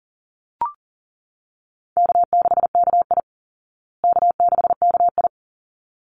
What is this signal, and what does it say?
Morse code, heard as a keyed tone, about 40 wpm, 700 Hz. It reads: K6KI K6KI
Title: Easy Call Signs Repeated 2x - 40wpm